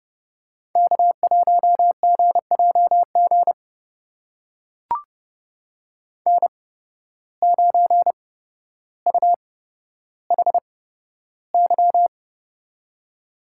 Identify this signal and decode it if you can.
Morse code, heard as a keyed tone, about 30 wpm, spaced as for 13 wpm, 700 Hz. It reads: K1GJG N9UHY
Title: USA Call Signs - 5 Characters - 8x Character Spacing - 30wpm